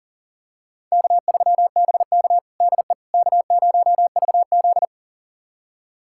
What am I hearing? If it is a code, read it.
Morse code, heard as a keyed tone, about 37 wpm, 700 Hz. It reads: K3BK DE K0VZ
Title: USA Callsign DE Callsign - 4 Characters - 40wpm